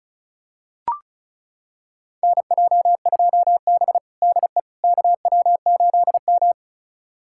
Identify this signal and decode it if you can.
Morse code, heard as a keyed tone, about 36 wpm, 700 Hz. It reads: NJ2B DE KW8M
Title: USA Callsign DE Callsign - 4 Characters - 35wpm